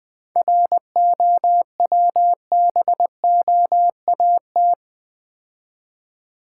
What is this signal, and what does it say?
Morse code, heard as a keyed tone, about 24 wpm, 700 Hz. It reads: ROWBOAT
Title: Easy Words - Speed-Racing with Hints - 20wpm